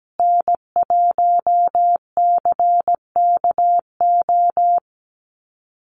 Morse code, heard as a keyed tone, about 17 wpm, 700 Hz. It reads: N1CKO
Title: USA Call Signs Repeated 3x 17wpm